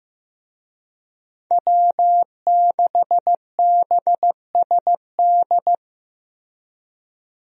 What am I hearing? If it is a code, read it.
Morse code, heard as a keyed tone, about 15 wpm, 700 Hz. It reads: W6BSD
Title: USA Call Signs - 5 Characters - Repeated 3x - 15wpm